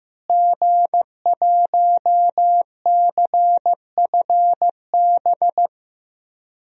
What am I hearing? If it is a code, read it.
Morse code, heard as a keyed tone, about 15 wpm, 700 Hz. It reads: G1CFB
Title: DXCC-100 Call Signs - 5 Characters - Repeated 2x - 15wpm